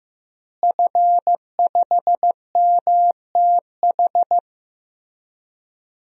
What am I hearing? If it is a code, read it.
Morse code, heard as a keyed tone, about 15 wpm, 700 Hz. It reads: F5MTH
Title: DXCC-100 Call Signs - 5 Characters - Repeated 2x - 15wpm